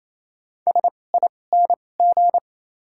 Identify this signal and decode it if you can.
Morse code, heard as a keyed tone, about 28 wpm, spaced as for 20 wpm, 700 Hz. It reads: SING
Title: 2x Character Spacing - Top 500 Words - 28wpm